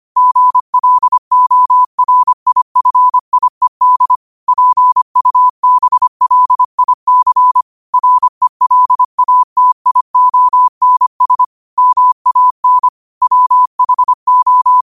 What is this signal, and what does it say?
Morse code, heard as a keyed tone, about 25 wpm, 1000 Hz. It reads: GLORIFIED PUBLIC RELATIONS MAN WHO
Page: New quotes every day in morse code at 25 Words per minute.